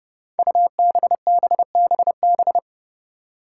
Morse code, heard as a keyed tone, about 30 wpm, 700 Hz. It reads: UBBBB
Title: Five Challenging Letter-Number Combinations 30wpm